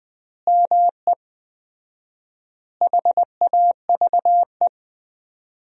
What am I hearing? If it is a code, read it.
Morse code, heard as a keyed tone, about 20 wpm, 700 Hz. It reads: ME HAVE
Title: 4x Word-Spacing - Sentences from Top 300 Words - 20wpm